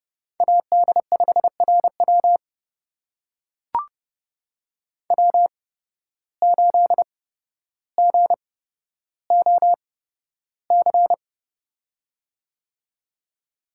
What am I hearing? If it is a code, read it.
Morse code, heard as a keyed tone, about 31 wpm, spaced as for 13 wpm, 700 Hz. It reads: AD5RW W8GOC
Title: USA Call Signs - 5 Characters - 8x Character Spacing - 30wpm